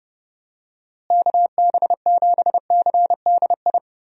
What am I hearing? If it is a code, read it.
Morse code, heard as a keyed tone, about 30 wpm, 700 Hz. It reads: KB7CDI
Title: USA Call Signs - 6 Characters - Repeated 3x - 30wpm